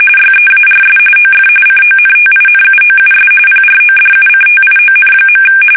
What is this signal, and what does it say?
RTTY, about 45.45 baud. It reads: YEARS OF CONTINUOUS PRODUCTION.
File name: fsk850 / teletype-corp